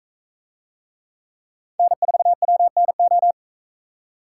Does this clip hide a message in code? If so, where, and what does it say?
Yes: Morse code from 1.8 to 3.3 s, heard as a keyed tone, about 40 wpm, 700 Hz. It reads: N4WNO